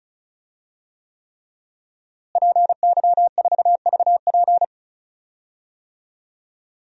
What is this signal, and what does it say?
Morse code, heard as a keyed tone, about 36 wpm, 700 Hz. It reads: PY4VP
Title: DXCC-100 Call Signs - 5 Characters - Repeated 3x - 35wpm